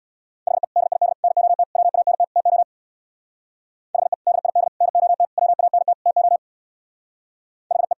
Morse code, heard as a keyed tone, about 60 wpm, 700 Hz. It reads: HE DID NOT WANT TO HE DID NOT WANT TO HE
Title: Speed-Racing - Sets of 5 Words 50wpm